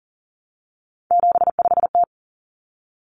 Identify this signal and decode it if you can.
Morse code, heard as a keyed tone, about 40 wpm, 700 Hz. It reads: 75T
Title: Common Cut Numbers - 3-Digit 40wpm